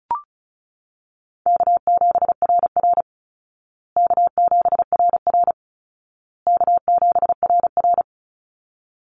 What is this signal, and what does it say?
Morse code, heard as a keyed tone, about 34 wpm, 700 Hz. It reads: K7RR K7RR K7RR
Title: Call Signs Repeated 2x - 35wpm